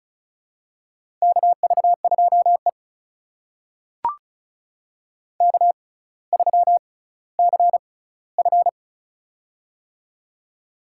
Morse code, heard as a keyed tone, about 34 wpm, spaced as for 18 wpm, 700 Hz. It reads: KV2E K3CF
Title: USA Call Signs - 4 Characters - 6x Character Spacing - 35wpm